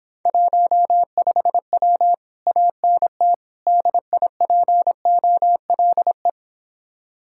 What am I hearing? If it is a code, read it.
Morse code, heard as a keyed tone, about 26 wpm, 700 Hz. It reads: 15W ANT DIPOLE